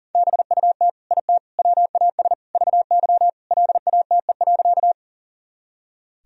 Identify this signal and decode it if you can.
Morse code, heard as a keyed tone, about 40 wpm, 700 Hz. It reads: BUT IT WAS VY LATE.
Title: Sentences from Top 300 Words and Abbreviations 40wpm